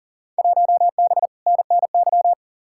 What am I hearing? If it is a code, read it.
Morse code, heard as a keyed tone, about 40 wpm, 700 Hz. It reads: 1B NNY
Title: Field Day Exchanges - 40wpm